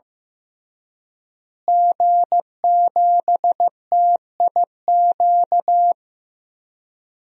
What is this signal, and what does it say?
Morse code, heard as a keyed tone, about 16 wpm, 700 Hz. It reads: G7TIQ
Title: DXCC-100 Call Signs - 5 Characters - Repeated 3x - 15wpm